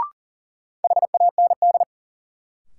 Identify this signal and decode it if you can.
Morse code, heard as a keyed tone, about 40 wpm, 700 Hz. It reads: HAND
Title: Sets of 3 Words - New Word Drill - 40wpm